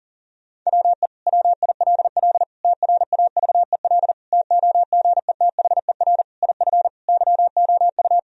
Morse code, heard as a keyed tone, about 40 wpm, 700 Hz. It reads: WE WILL TRAVEL TOGETHER IF YOU
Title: Sentences from Top 1000 Words - Review Part 2 - 40wpm